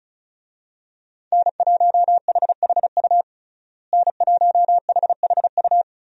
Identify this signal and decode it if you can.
Morse code, heard as a keyed tone, about 36 wpm, 700 Hz. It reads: N1HHU N1HHU
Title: USA Call Signs - 5 Characters - Repeated 3x - 35wpm